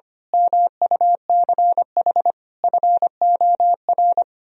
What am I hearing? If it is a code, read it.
Morse code, heard as a keyed tone, about 24 wpm, 700 Hz. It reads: MUCH FOR
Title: Speed-Racing - Sets of 4 Words 25wpm